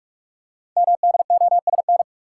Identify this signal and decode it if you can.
Morse code, heard as a keyed tone, about 44 wpm, 700 Hz. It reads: MDOSN
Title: Code Groups - 5 Letter:Numbers - 45wpm